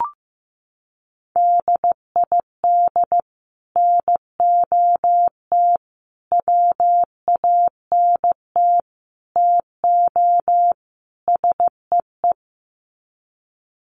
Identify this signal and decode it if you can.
Morse code, heard as a keyed tone, about 15 wpm, 700 Hz. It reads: DID NOT WANT TO SEE
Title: Sets of 5 Words Encore2 15wpm